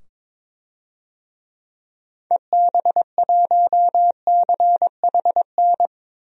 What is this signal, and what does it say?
Morse code, heard as a keyed tone, about 22 wpm, 700 Hz. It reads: EB1CHN
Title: DXCC-100 Call Signs - 6 Characters - 22wpm